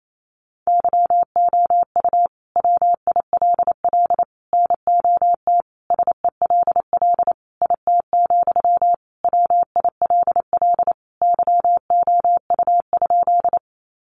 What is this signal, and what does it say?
Morse code, heard as a keyed tone, about 28 wpm, 700 Hz. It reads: YOU WILL NOT SELL IT, WILL YOU?
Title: Sentences from Top 1100 Words 28wpm